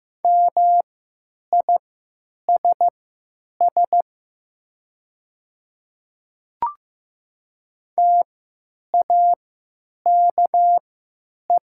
Morse code, heard as a keyed tone, about 15 wpm, spaced as for 9 wpm, 700 Hz. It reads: MISS TAKE
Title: 3x Character Spacing - Top 500 Words - 15wpm